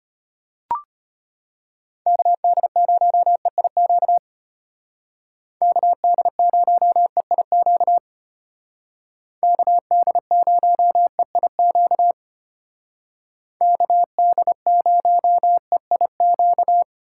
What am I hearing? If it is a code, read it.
Morse code, heard as a keyed tone, about 30 wpm, 700 Hz. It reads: KD0EIQ KD0EIQ KD0EIQ KD0EIQ
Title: Speed Racing - USA 2x3 Call Signs 25wpm